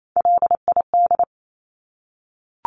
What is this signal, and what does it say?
Morse code, heard as a keyed tone, about 28 wpm, 700 Hz. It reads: LID
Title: Abbreviations Repeated 3x - 28wpm